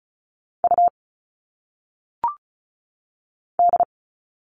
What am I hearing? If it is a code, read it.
Morse code, heard as a keyed tone, about 36 wpm, 700 Hz. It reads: U D
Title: All Cut Numbers - 1-Digit 35wpm